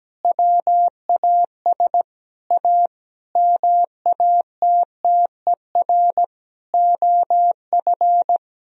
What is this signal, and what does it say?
Morse code, heard as a keyed tone, about 17 wpm, 700 Hz. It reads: WAS A MATTER OF
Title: Sentences from Top 1000 Words - Review Part 4 - 17wpm